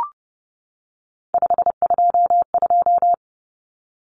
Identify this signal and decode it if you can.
Morse code, heard as a keyed tone, about 30 wpm, 700 Hz. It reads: 522
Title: Numbers - 3-Digit 30wpm